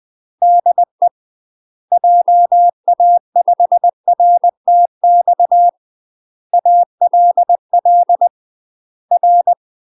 Morse code, heard as a keyed tone, about 20 wpm, 700 Hz. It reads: DE JA5RTX ALL R
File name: UK MASTER QSOs TRAINING/20WPM